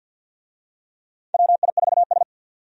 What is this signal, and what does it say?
Morse code, heard as a keyed tone, about 53 wpm, 700 Hz. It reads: WI4S
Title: USA Call Signs - 4 Characters - 50wpm